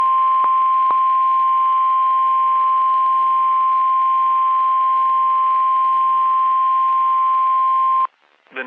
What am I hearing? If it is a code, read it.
Emergency Alert System (SAME) tones (the 1050 Hz attention tone). It is heard from 0.9 to 8.1 s.
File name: eas messages / EAS WXM31 Elmira